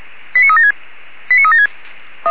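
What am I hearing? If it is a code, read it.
Five-tone selective call (ZVEI), heard as quick runs of five tones, tones of about 70 ms. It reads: 89376 89376
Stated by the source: ZVEI five-tone call received by SDR, fire brigade Orlamünde (feuerwehr-orlamuende).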